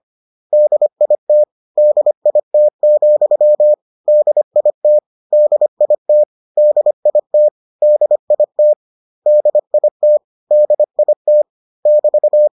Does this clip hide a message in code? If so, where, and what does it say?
Yes: Morse code from 0.5 to 12.5 s, heard as a keyed tone, about 25 wpm, 600 Hz. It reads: DIT DIT, DIT DIT DIT DIT DIT DIT <BT>